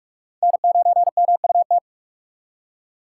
Morse code, heard as a keyed tone, about 56 wpm, 700 Hz. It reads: N9MUT